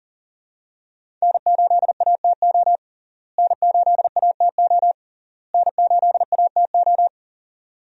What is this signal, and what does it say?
Morse code, heard as a keyed tone, about 40 wpm, 700 Hz. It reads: N8ATO N8ATO N8ATO
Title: USA Call Signs - 5 Characters - Repeated 3x - 40wpm